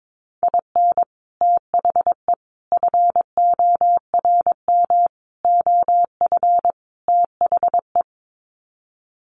Morse code, heard as a keyed tone, about 22 wpm, 700 Hz. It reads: IN THE FORM OF THE
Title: Sets of 5 Words Encore2 22wpm